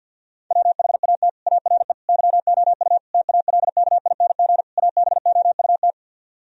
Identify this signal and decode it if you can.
Morse code, heard as a keyed tone, about 64 wpm, 700 Hz. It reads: WHAT ARE YOU TALKING ABOUT